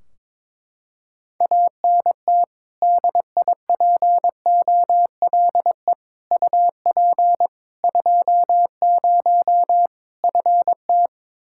Morse code, heard as a keyed tone, about 22 wpm, 700 Hz. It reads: ANT DIPOLE UP 20 FT
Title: Practice QSO - Exchange 2 - Info - 22wpm